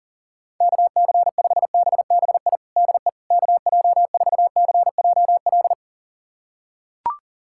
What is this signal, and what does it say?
Morse code, heard as a keyed tone, about 40 wpm, 700 Hz. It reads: KC5BBI DE KJ4CJL
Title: USA Callsign DE Callsign - 6 Characters - 40wpm